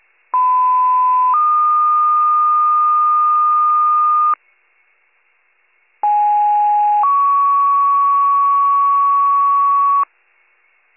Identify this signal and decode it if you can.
Two-tone page, heard as a one-second tone, then a three-second tone, about 1010 then 1220 Hz, about 830 then 1120 Hz. It is heard from 0.3 to 10.0 s.